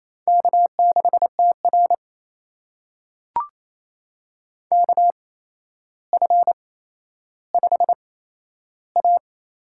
Morse code, heard as a keyed tone, about 28 wpm, spaced as for 12 wpm, 700 Hz. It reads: K6TR KF5A
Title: USA Call Signs - 8x Character Spacing - 28wpm